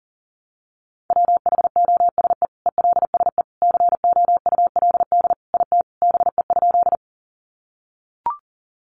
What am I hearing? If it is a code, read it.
Morse code, heard as a keyed tone, about 40 wpm, 700 Hz. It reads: WHOSE ELSE COULD IT BE?
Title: Sentences from Top 1200 Words 40wpm